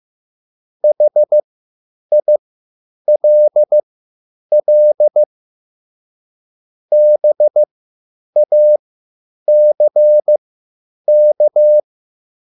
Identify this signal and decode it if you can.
Morse code, heard as a keyed tone, about 15 wpm, spaced as for 9 wpm, 600 Hz. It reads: HILL BACK